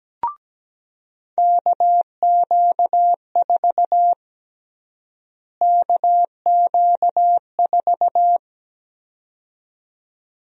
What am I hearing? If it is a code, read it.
Morse code, heard as a keyed tone, about 17 wpm, 700 Hz. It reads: KQ4 KQ4
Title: USA Call Sign Prefixes - Repeated 2x - 17wpm